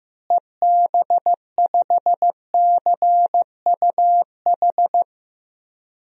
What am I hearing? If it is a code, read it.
Morse code, heard as a keyed tone, about 15 wpm, 700 Hz. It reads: EB5CUH
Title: DXCC-100 Call Signs - 6 Characters - Repeated 3x - 15wpm